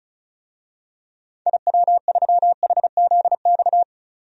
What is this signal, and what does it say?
Morse code, heard as a keyed tone, about 36 wpm, 700 Hz. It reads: IW3HZX
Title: DXCC-100 Call Signs - 6 Characters - 35wpm